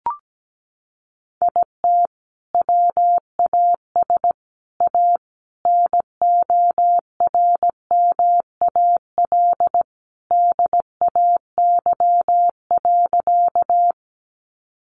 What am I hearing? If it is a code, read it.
Morse code, heard as a keyed tone, about 17 wpm, 700 Hz. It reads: IT WAS A NORMAL DAY.
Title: Sentences from Top 1200 Words 17wpm